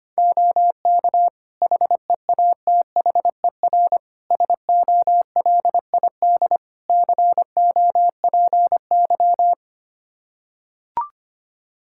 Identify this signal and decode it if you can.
Morse code, heard as a keyed tone, about 25 wpm, 700 Hz. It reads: OK HEATHER SOLID COPY
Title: K1USN SST QSO Practice 25wpm